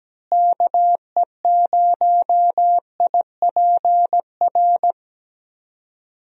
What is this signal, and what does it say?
Morse code, heard as a keyed tone, about 17 wpm, 700 Hz. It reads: KE0IPR
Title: USA Call Signs - 6 Characters - Repeated 3x - 17wpm